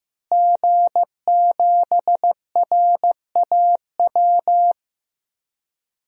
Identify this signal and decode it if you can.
Morse code, heard as a keyed tone, about 16 wpm, 700 Hz. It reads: G7RAW
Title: DXCC-100 Call Signs - 5 Characters - Repeated 3x - 15wpm